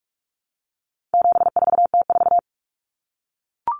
Morse code, heard as a keyed tone, about 44 wpm, 700 Hz. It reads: 74T4
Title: Common Cut Numbers - 4-Digit 45wpm